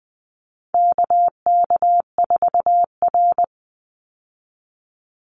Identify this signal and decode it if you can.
Morse code, heard as a keyed tone, about 20 wpm, 700 Hz. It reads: KK4R
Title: Call Signs - 20wpm